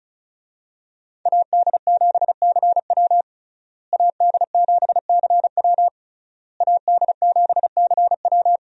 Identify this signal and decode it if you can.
Morse code, heard as a keyed tone, about 36 wpm, 700 Hz. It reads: AD7CW AD7CW AD7CW
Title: USA Call Signs - 5 Characters - Repeated 3x - 35wpm